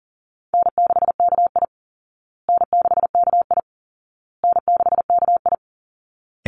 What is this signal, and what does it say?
Morse code, heard as a keyed tone, about 40 wpm, 700 Hz. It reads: N6KI N6KI N6KI
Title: Easy Call Signs Repeated 3x - 40wpm